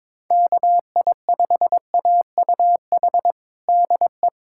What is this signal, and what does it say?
Morse code, heard as a keyed tone, about 22 wpm, 700 Hz. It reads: KI5AUH DE
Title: USA Callsign DE Callsign 22wpm